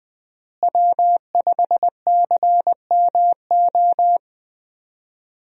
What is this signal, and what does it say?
Morse code, heard as a keyed tone, about 20 wpm, 700 Hz. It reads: W5CMO
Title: USA Call Signs - 5 Characters - Repeated 2x - 20wpm